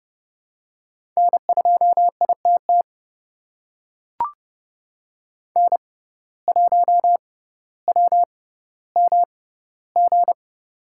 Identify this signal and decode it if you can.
Morse code, heard as a keyed tone, about 31 wpm, spaced as for 16 wpm, 700 Hz. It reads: N2ITT N1WMG
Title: USA Call Signs - 5 Characters - 6x Character Spacing - 30wpm